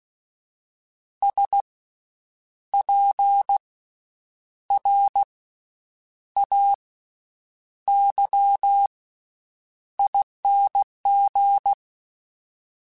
Morse code, heard as a keyed tone, about 16 wpm, 800 Hz. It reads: S P R A Y ING
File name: Special INGs-mp3s / Special INGs-mp3s / Slow ing @15